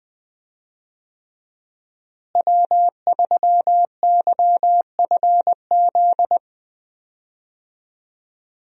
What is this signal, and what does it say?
Morse code, heard as a keyed tone, about 20 wpm, 700 Hz. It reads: W3YFZ
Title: USA Call Signs - 5 Characters - Repeated 3x - 20wpm